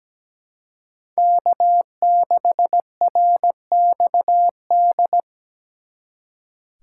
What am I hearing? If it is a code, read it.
Morse code, heard as a keyed tone, about 20 wpm, 700 Hz. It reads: K6RXD
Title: Speed Racing - USA 1x3 Call Signs 17wpm